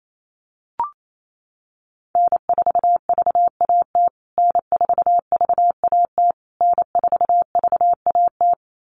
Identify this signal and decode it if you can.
Morse code, heard as a keyed tone, about 28 wpm, 700 Hz. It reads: N4VAT N4VAT N4VAT
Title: USA Call Signs Repeated 3x 28wpm